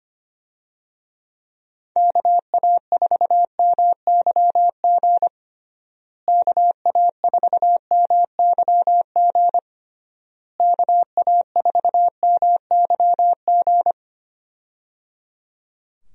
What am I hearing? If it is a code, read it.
Morse code, heard as a keyed tone, about 25 wpm, 700 Hz. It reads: KA4MYG KA4MYG KA4MYG
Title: USA Call Signs - 6 Characters - Repeated 3x - 25wpm